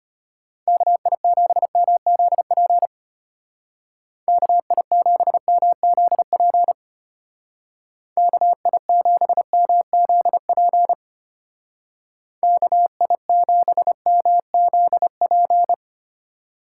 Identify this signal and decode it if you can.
Morse code, heard as a keyed tone, about 33 wpm, 700 Hz. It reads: KI7MZP KI7MZP KI7MZP KI7MZP